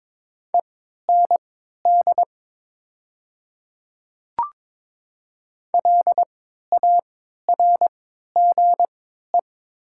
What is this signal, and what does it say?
Morse code, heard as a keyed tone, about 23 wpm, spaced as for 13 wpm, 700 Hz. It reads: END LARGE
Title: 3x Character Spacing - Top 200 Words - 22wpm